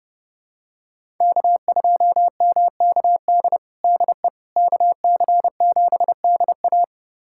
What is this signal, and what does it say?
Morse code, heard as a keyed tone, about 31 wpm, 700 Hz. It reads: K2MKD DE KC7DA
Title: USA Callsign DE Callsign - 5 Characters - 30wpm